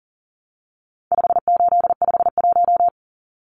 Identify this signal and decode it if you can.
Morse code, heard as a keyed tone, about 40 wpm, 700 Hz. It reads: 5851
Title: Numbers - 4-Digit 40wpm